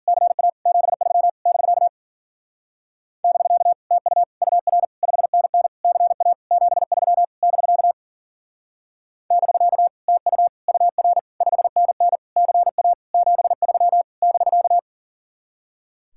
Morse code, heard as a keyed tone, about 52 wpm, 700 Hz. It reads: CA 73 <BK> <BK> TU UR 5NN CA 73 <BK> <BK> TU UR 5NN CA 73 <BK>